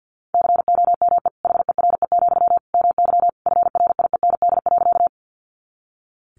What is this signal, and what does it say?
Morse code, heard as a keyed tone, about 50 wpm, 700 Hz. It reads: COME HERE, MY FRIEND.
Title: Sentences from Top 300 Words 50wpm